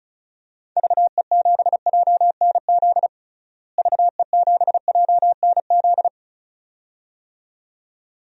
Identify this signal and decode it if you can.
Morse code, heard as a keyed tone, about 36 wpm, 700 Hz. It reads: VE7JNZ VE7JNZ
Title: DXCC-100 Call Signs - 6 Characters - Repeated 2x - 35wpm